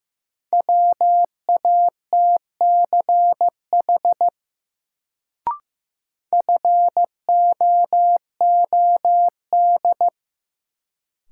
Morse code, heard as a keyed tone, about 15 wpm, 700 Hz. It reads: WATCH FOOD
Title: Sentences from Top 300 Words - New Word Drill - 15wpm